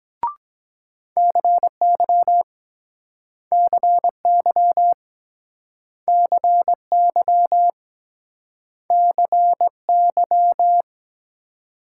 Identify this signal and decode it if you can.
Morse code, heard as a keyed tone, about 24 wpm, 700 Hz. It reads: CY CY CY CY